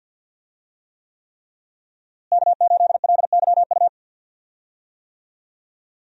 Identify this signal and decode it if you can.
Morse code, heard as a keyed tone, about 47 wpm, 700 Hz. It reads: K8LYU
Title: USA Call Signs - 5 Characters - Repeated 3x - 50wpm